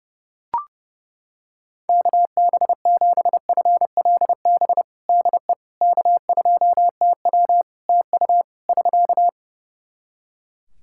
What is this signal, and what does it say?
Morse code, heard as a keyed tone, about 30 wpm, 700 Hz. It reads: KB7FLB DE K2TW TU <SK>
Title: Practice QSO - Exchange 4 - Ending - 30wpm